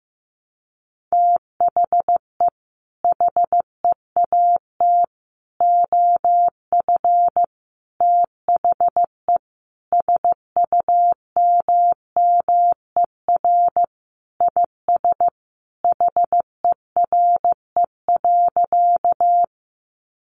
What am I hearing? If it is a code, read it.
Morse code, heard as a keyed tone, about 14 wpm, 700 Hz. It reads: THE HEAT OF THE SUMMER IS HERE.
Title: Sentences from Top 1100 Words 15wpm